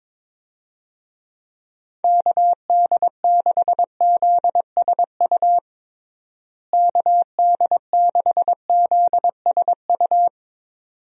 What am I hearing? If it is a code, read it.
Morse code, heard as a keyed tone, about 22 wpm, 700 Hz. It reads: KD6ZSU KD6ZSU
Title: USA Call Signs - 6 Characters - Repeated 2x - 22wpm